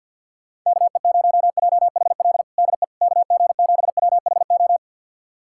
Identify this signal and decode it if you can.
Morse code, heard as a keyed tone, about 50 wpm, 700 Hz. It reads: KE0JHL DE KG7WHO
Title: USA Callsign DE Callsign 50wpm